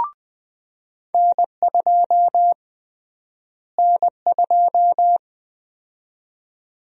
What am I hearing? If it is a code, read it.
Morse code, heard as a keyed tone, about 20 wpm, 700 Hz. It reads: N2 N2
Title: USA Call Sign Prefixes - Repeated 2x - 20wpm